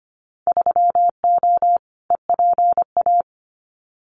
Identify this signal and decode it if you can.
Morse code, heard as a keyed tone, about 25 wpm, 700 Hz. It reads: 3O EPA
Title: Winter Field Day Exchanges 25wpm